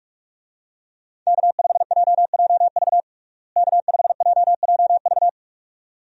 Morse code, heard as a keyed tone, about 41 wpm, 700 Hz. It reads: K5JJV K5JJV
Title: USA Call Signs - 5 Characters - Repeated 2x - 45wpm